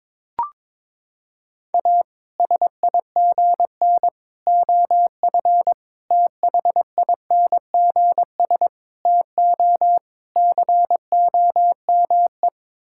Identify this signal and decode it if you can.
Morse code, heard as a keyed tone, about 22 wpm, 700 Hz. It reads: A SIGN OF THINGS TO COME
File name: practice / 22wpm / Sets of 6 Words - 22wpm